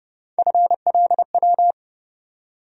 Morse code, heard as a keyed tone, about 30 wpm, 700 Hz. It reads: FLW
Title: USA Call Sign Postfixes - 3 Letters 30wpm